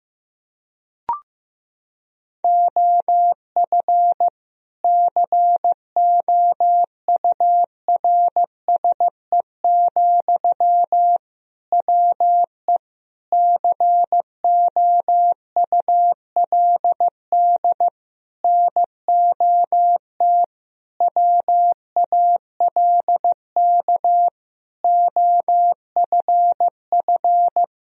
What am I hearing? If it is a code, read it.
Morse code, heard as a keyed tone, about 15 wpm, 700 Hz. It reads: OF COURSE, WE COULD NOT WALK OFF
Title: Sentences from Top 1000 Words - Review Part 2 - 15wpm